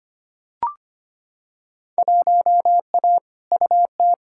Morse code, heard as a keyed tone, about 25 wpm, 700 Hz. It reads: 1A UT
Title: Field Day Exchanges - 25wpm